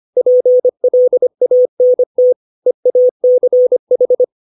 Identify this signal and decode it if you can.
Morse code, heard as a keyed tone, about 25 wpm, 500 Hz. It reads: PLANT EACH
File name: 632 words / 632 Words 25WPM